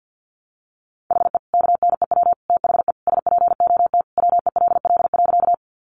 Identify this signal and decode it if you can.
Morse code, heard as a keyed tone, about 50 wpm, 700 Hz. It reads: HE KNEW THE SPOT WELL.
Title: Sentences from Top 1100 Words 50wpm